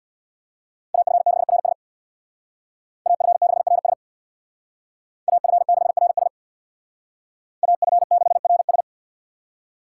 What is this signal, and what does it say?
Morse code, heard as a keyed tone, about 61 wpm, 700 Hz. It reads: AF6RS AF6RS AF6RS AF6RS
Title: Speed Racing - USA 2x2 Call Signs 50wpm